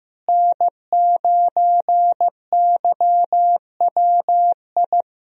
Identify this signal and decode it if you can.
Morse code, heard as a keyed tone, about 15 wpm, 700 Hz. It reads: N9YWI
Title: USA Call Signs - 5 Characters - Repeated 3x - 15wpm